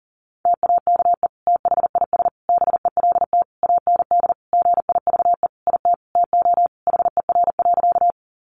Morse code, heard as a keyed tone, about 40 wpm, 700 Hz. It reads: TAKE THIS BELT AND GIVE IT TO HER.
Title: Sentences from Top 1400 Words 40wpm